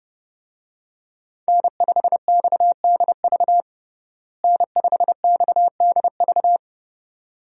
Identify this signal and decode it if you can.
Morse code, heard as a keyed tone, about 32 wpm, 700 Hz. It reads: N5XDV N5XDV
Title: USA Call Signs - 5 Characters - Repeated 2x - 30wpm